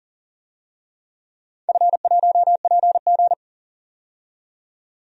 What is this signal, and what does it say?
Morse code, heard as a keyed tone, about 40 wpm, 700 Hz. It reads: F1PG
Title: DXCC-100 Call Signs - 4 Characters - 40wpm